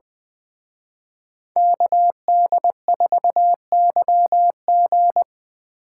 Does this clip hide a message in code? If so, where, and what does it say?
Yes: Morse code from 1.6 to 5.2 s, heard as a keyed tone, about 20 wpm, 700 Hz. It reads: KD4YG